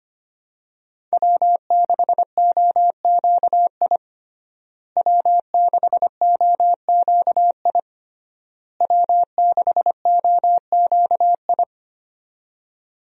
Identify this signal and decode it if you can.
Morse code, heard as a keyed tone, about 25 wpm, 700 Hz. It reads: W6OQI W6OQI W6OQI
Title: USA Call Signs - 5 Characters - Repeated 3x - 25wpm